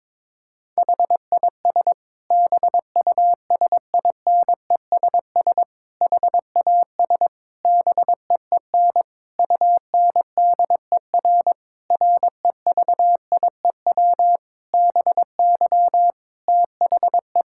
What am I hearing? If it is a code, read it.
Morse code, heard as a keyed tone, about 22 wpm, 700 Hz. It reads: HIS BUSINESS HAS BEEN UNDER REVIEW BY THE
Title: Sentences from Top 1000 Words - Review Part 6 - 22wpm